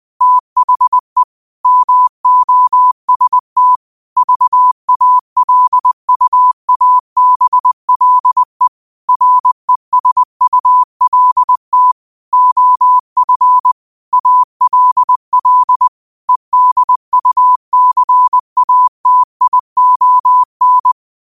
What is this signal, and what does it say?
Morse code, heard as a keyed tone, about 20 wpm, 1000 Hz. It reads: THE MOST VALUABLE RESULT OF ALL EDUCATION
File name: qotdfiles / 20wpm